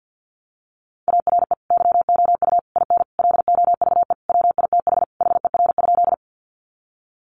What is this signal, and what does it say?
Morse code, heard as a keyed tone, about 50 wpm, 700 Hz. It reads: ARE YOU IN LOVE WITH HER?
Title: Sentences from Top 400 Words 50wpm